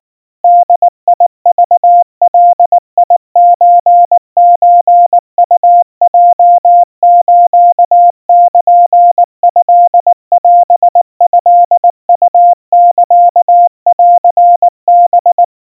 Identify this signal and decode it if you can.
Morse code (Japanese Wabun code), heard as a keyed tone, about 19 wpm, 700 Hz. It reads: ボクガソソウヲスルトオトウサンハ
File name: kawa / Morse sample / kamyu 70 cpm 700 Hz Crispy